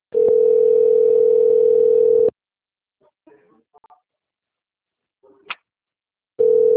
Ringback tone (North America), 440 + 480 Hz.
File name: rg-1600-unknown- - - .45025